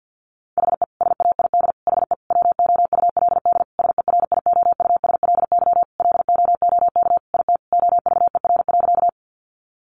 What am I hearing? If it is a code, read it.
Morse code, heard as a keyed tone, about 50 wpm, 700 Hz. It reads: HE SAID HE WOULD SERIOUSLY LOOK IT OVER.
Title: Sentences from Top 1200 Words 50wpm